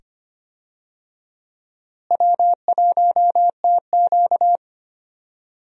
Morse code, heard as a keyed tone, about 25 wpm, 700 Hz. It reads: W1TQ
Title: USA Call Signs - 4 Characters - Repeated 2x - 25wpm